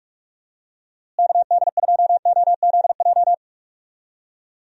Morse code, heard as a keyed tone, about 42 wpm, 700 Hz. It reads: KD2OZJ
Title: USA Call Signs - 6 Characters - Repeated 2x - 45wpm